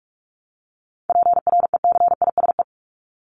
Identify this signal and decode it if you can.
Morse code, heard as a keyed tone, about 44 wpm, 700 Hz. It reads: PRECISE
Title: LY Suffix - Top 100 Words with Base Words 45wpm